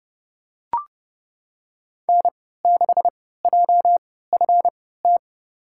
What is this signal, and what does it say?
Morse code, heard as a keyed tone, about 31 wpm, spaced as for 17 wpm, 700 Hz. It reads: N6JFT
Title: USA Call Signs - 5 Characters - 3x Character Spacing - 30wpm